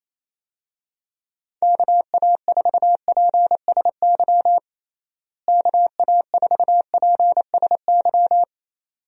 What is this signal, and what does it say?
Morse code, heard as a keyed tone, about 28 wpm, 700 Hz. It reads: KA4PSY KA4PSY
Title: USA Call Signs - 6 Characters - Repeated 2x - 28wpm